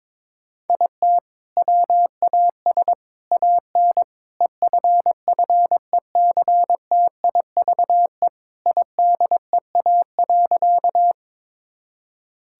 Morse code, heard as a keyed tone, about 22 wpm, 700 Hz. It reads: IT WAS AN EFFECTIVE IDEA.
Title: Sentences from Top 1000 Words - Review Part 6 - 22wpm